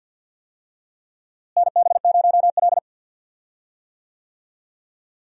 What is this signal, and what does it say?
Morse code, heard as a keyed tone, about 46 wpm, 700 Hz. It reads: NB0L
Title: USA Call Signs - 4 Characters - Repeated 2x - 50wpm